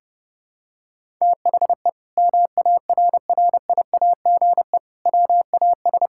Morse code, heard as a keyed tone, about 32 wpm, 700 Hz. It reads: THE MARRIAGE WAS
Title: Sentences from Top 1000 Words - Review Part 5 - 30wpm